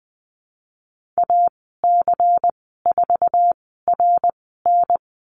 Morse code, heard as a keyed tone, about 20 wpm, spaced as for 15 wpm, 700 Hz. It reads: AC4RN
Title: 122 Lesson 20g - Number 3 - Call Signs 20wpm